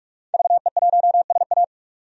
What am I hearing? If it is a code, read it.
Morse code, heard as a keyed tone, about 43 wpm, 700 Hz. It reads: VE1SA
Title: DXCC-100 Call Signs - 5 Characters - 45wpm